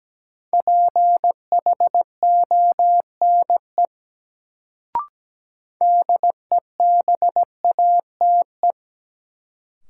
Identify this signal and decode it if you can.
Morse code, heard as a keyed tone, about 17 wpm, 700 Hz. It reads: PHONE DEBATE
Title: Sentences from Top 1100 Words - New Word Drill - 17wpm